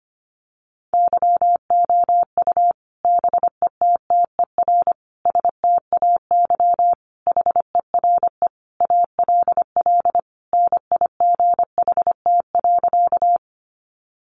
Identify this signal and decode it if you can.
Morse code, heard as a keyed tone, about 25 wpm, 700 Hz. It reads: YOU BETTER STAY HERE ALL NIGHT.
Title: Sentences from Top 500 Words 25wpm